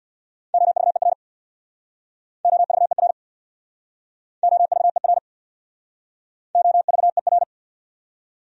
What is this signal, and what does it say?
Morse code, heard as a keyed tone, about 67 wpm, 700 Hz. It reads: OVER OVER OVER OVER